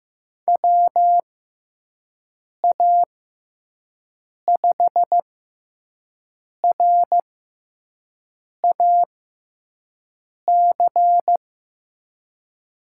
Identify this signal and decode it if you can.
Morse code, heard as a keyed tone, about 15 wpm, spaced as for 5 wpm, 700 Hz. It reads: WA5RAC
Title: USA Call Signs - 6 Characters - 6x Character Spacing - 15wpm